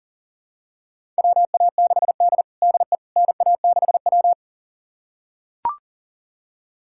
Morse code, heard as a keyed tone, about 40 wpm, 700 Hz. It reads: WA6D DE NA6W
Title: USA Callsign DE Callsign - 4 Characters - 40wpm